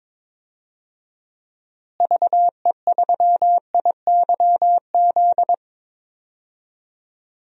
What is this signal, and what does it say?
Morse code, heard as a keyed tone, about 22 wpm, 700 Hz. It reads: VE3IYZ
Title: DXCC-100 Call Signs - 6 Characters - Repeated 3x - 22wpm